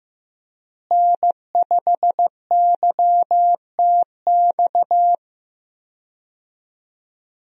Morse code, heard as a keyed tone, about 15 wpm, 700 Hz. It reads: N5YTX
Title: USA Call Signs - 5 Characters - Repeated 3x - 15wpm